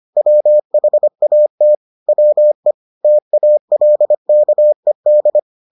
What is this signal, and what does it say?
Morse code, heard as a keyed tone, about 25 wpm, 600 Hz. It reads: WHAT WE TALKED